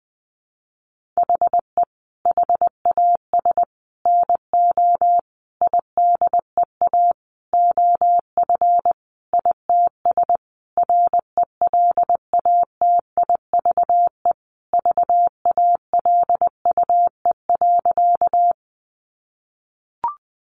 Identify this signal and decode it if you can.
Morse code, heard as a keyed tone, about 20 wpm, 700 Hz. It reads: HE HAS NO IDEA OF ITS RELATIVE VALUE.
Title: Sentences from Top 1400 Words 20wpm